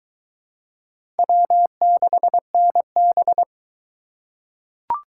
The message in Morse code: W6NB

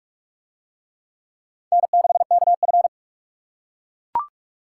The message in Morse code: N6KF

